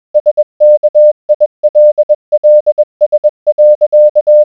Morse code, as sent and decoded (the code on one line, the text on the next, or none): ... -.- .. .-.. .-.. ... .-.-.-
SKILLS.